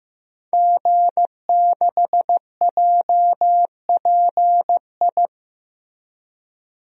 Morse code, heard as a keyed tone, about 15 wpm, 700 Hz. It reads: G6JPI
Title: DXCC-100 Call Signs - 5 Characters - Repeated 3x - 15wpm